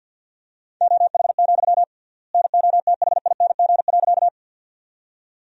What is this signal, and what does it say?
Morse code, heard as a keyed tone, about 49 wpm, 700 Hz. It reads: OH, NOTHING.
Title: 3x Word-Spacing - Sentences from Top 500 Words - 50wpm